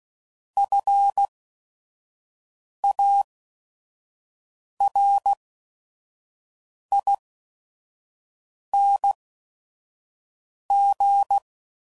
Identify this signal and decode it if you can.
Morse code, heard as a keyed tone, about 16 wpm, spaced as for 5 wpm, 800 Hz. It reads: FARING
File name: Suffixes / Special ING's / VerySlowToFaster ING's